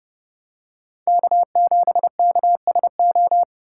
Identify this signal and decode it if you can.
Morse code, heard as a keyed tone, about 28 wpm, 700 Hz. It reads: K7KSO